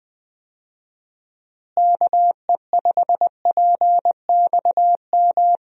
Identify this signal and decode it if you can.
Morse code, heard as a keyed tone, about 20 wpm, 700 Hz. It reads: KE5PXM
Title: USA Call Signs - 6 Characters - Repeated 3x - 20wpm